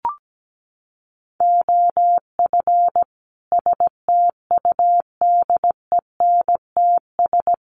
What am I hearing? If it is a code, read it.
Morse code, heard as a keyed tone, about 17 wpm, 700 Hz. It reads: OF STUDENTS
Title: Sets of 5 Words Encore 17wpm